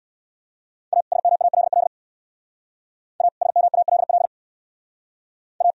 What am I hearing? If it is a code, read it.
Morse code, heard as a keyed tone, about 63 wpm, 700 Hz. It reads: A SMALL A SMALL A